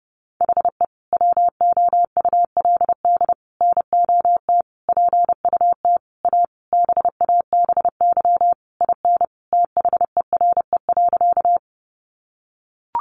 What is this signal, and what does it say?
Morse code, heard as a keyed tone, about 30 wpm, 700 Hz. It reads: HE WOULD NOT PUT A BABY IN THERE.
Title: Easy Phrases 30wpm